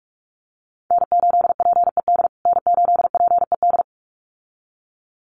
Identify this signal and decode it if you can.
Morse code, heard as a keyed tone, about 44 wpm, 700 Hz. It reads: N8PED N8PED
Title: USA Call Signs Repeated 2x 45wpm